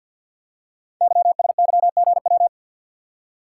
Morse code, heard as a keyed tone, about 48 wpm, 700 Hz. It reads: YSYGW
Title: Code Groups - 5 Letters - 50wpm